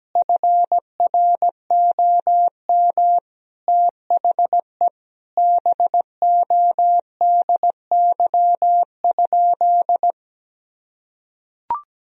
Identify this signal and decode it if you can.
Morse code, heard as a keyed tone, about 17 wpm, 700 Hz. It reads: FROM THE BODY?
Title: Sentences from Top 1000 Words - Review Part 2 - 17wpm